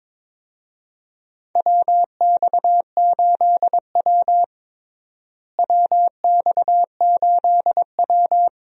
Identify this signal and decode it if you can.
Morse code, heard as a keyed tone, about 22 wpm, 700 Hz. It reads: WX8W WX8W
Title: USA Call Signs - 4 Characters - Repeated 3x - 22wpm